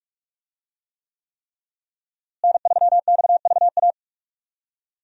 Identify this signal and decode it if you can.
Morse code, heard as a keyed tone, about 44 wpm, 700 Hz. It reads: N3XVA
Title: USA Call Signs - 5 Characters - Repeated 3x - 45wpm